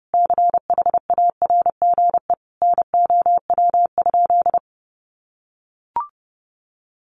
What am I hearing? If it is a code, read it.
Morse code, heard as a keyed tone, about 30 wpm, 700 Hz. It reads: CHARGE NOW?
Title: Sentences from Top 1200 Words 30wpm